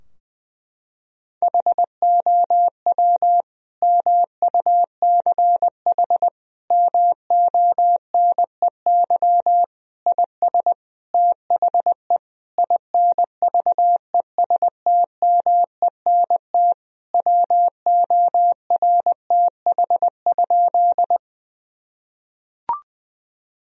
Morse code, heard as a keyed tone, about 20 wpm, 700 Hz. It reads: HOW MUCH MONEY IS THE INVESTMENT WORTH?
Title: Sentences from Top 1000 Words - Review Part 4 - 20wpm